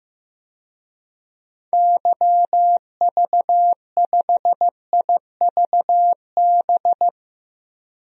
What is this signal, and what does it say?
Morse code, heard as a keyed tone, about 15 wpm, 700 Hz. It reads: YV5IVB
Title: DXCC-100 Call Signs - 6 Characters - Repeated 3x - 15wpm